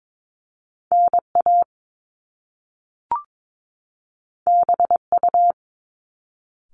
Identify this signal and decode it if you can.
Morse code, heard as a keyed tone, about 22 wpm, 700 Hz. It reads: NA BU
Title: Top Two Letter Combinations in Words 22wpm